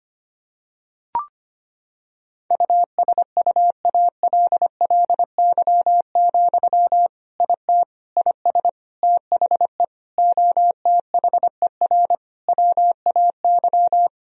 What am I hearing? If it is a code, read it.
Morse code, heard as a keyed tone, about 25 wpm, 700 Hz. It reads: USUALLY, IT IS THE OTHER WAY
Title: Sentences from Top 1000 Words - Review Part 3 - 25wpm